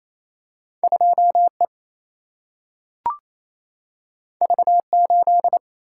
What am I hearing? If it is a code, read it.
Morse code, heard as a keyed tone, about 28 wpm, 700 Hz. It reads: 2E V8
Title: Single-Letter Single-Number 28wpm